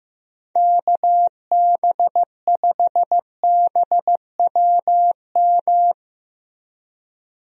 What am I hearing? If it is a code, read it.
Morse code, heard as a keyed tone, about 15 wpm, 700 Hz. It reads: KB5BWM
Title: USA Call Signs - 6 Characters - Repeated 2x - 15wpm